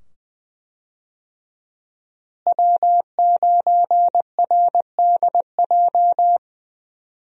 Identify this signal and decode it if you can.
Morse code, heard as a keyed tone, about 20 wpm, 700 Hz. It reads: W9RDJ
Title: USA Call Signs - 5 Characters - Repeated 2x - 20wpm